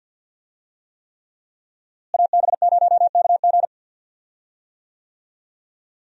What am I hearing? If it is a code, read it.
Morse code, heard as a keyed tone, about 52 wpm, 700 Hz. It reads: AB0KG
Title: USA Call Signs - 5 Characters - Repeated 3x - 50wpm